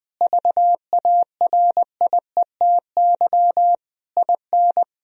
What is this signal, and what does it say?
Morse code, heard as a keyed tone, about 20 wpm, 700 Hz. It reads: VARIETY IN
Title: Sentences from Top 1000 Words - Review Part 4 - 20wpm